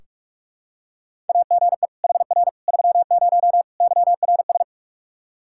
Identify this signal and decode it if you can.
Morse code, heard as a keyed tone, about 46 wpm, 700 Hz. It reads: AGE HR 30 YRS
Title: Practice QSO - Exchange 3 - More Info - 45wpm